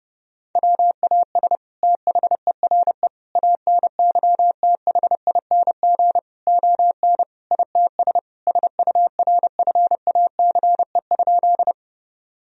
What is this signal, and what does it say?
Morse code, heard as a keyed tone, about 30 wpm, 700 Hz. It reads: WAS THERE ANYTHING ON ITS SURFACE?
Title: Sentences from Top 1000 Words - Review Part 3 - 30wpm